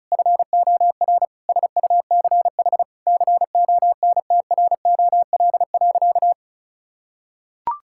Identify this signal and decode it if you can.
Morse code, heard as a keyed tone, about 36 wpm, 700 Hz. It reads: FOR SUCH CONTROL.
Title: Sentences from Top 1000 Words - Review Part 5 - 35wpm